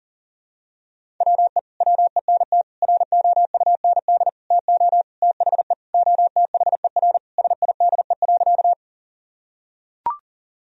Morse code, heard as a keyed tone, about 40 wpm, 700 Hz. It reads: WE WENT ROUND TO THE OTHER SIDE.
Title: Sentences from Top 1000 Words - Review Part 1 - 40wpm